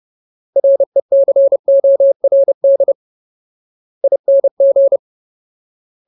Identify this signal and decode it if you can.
Morse code, heard as a keyed tone, about 30 wpm, spaced as for 21 wpm, 550 Hz. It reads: RECORDING
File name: Ultimate Word List / 30WPM Ultimate